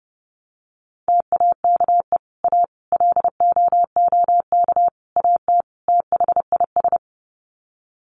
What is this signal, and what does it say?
Morse code, heard as a keyed tone, about 30 wpm, 700 Hz. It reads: TAKE A LOOK AT THIS
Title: Sets of 5 Words 30wpm